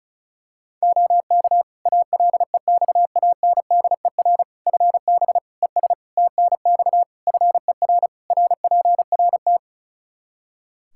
Morse code, heard as a keyed tone, about 36 wpm, 700 Hz. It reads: OK ALEXANDER FB ES TNX FER RPRT
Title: K1USN SST QSO Practice 35wpm